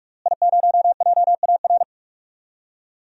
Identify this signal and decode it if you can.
Morse code, heard as a keyed tone, about 46 wpm, 700 Hz. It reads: I0JAR
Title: DXCC-100 Call Signs - 5 Characters - Repeated 3x - 45wpm